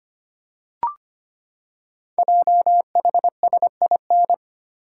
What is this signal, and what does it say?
Morse code, heard as a keyed tone, about 25 wpm, 700 Hz. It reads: JHSIN
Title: Code Groups - 5 Letters - 25wpm